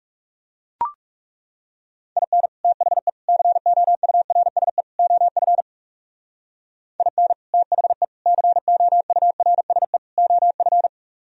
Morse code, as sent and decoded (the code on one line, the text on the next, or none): .. -. / - .... / -.-. --- ..- .-. ... / --- ..-. / .. -. / - .... / -.-. --- ..- .-. ... / --- ..-.
IN THE COURSE OF IN THE COURSE OF